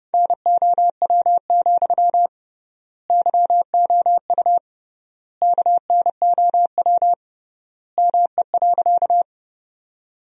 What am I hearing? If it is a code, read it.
Morse code, heard as a keyed tone, about 29 wpm, 700 Hz. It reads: NOW, YOU KNOW ME.
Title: 3x Word-Spacing - Sentences from Top 200 Words - 30wpm